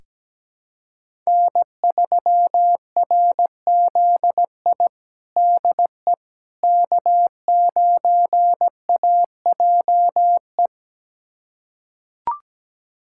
Morse code, heard as a keyed tone, about 17 wpm, 700 Hz. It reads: N3RZI DE K9AJE
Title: USA Callsign DE Callsign - 5 Characters - 17wpm